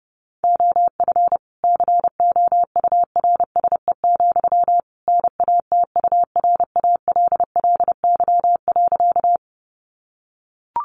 Morse code, heard as a keyed tone, about 30 wpm, 700 Hz. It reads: OF COURSE, NATURALLY.
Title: Sentences from Top 1300 Words 30wpm